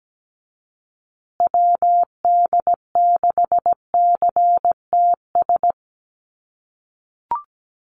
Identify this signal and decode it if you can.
Morse code, heard as a keyed tone, about 17 wpm, 700 Hz. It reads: WD6CTS
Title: USA Call Signs Repeated 3x 17wpm